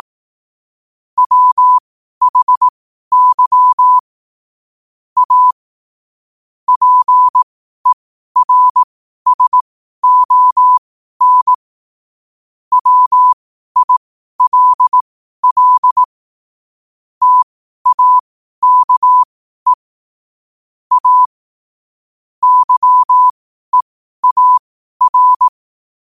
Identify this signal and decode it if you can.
Morse code, heard as a keyed tone, about 18 wpm, spaced as for 12 wpm, 1000 Hz. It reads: WHY A PERSON WILL TAKE A YEAR
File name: qotdfiles / 12wpm